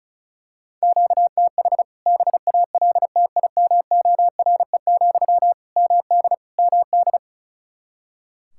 Morse code, heard as a keyed tone, about 34 wpm, 700 Hz. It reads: QTH BALTIMORE, MD MD
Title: K1USN SST QSO Practice 35wpm